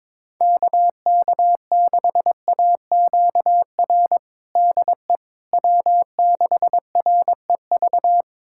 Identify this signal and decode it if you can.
Morse code, heard as a keyed tone, about 22 wpm, 700 Hz. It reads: KK6AQR DE W6REV
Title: USA Callsign DE Callsign 22wpm